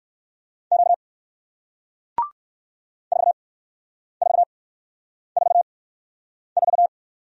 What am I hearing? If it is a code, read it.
Morse code, heard as a keyed tone, about 56 wpm, spaced as for 9 wpm, 700 Hz. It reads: Q 4444
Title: Speed-Racing Single Letter-Number 45wpm